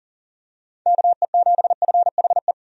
Morse code, heard as a keyed tone, about 40 wpm, 700 Hz. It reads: KE7FHE
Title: USA Call Signs - 6 Characters - 40wpm